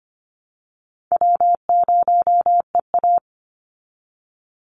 Morse code, heard as a keyed tone, about 25 wpm, 700 Hz. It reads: W0EA
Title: Easy Call Signs Repeated 3x - 25wpm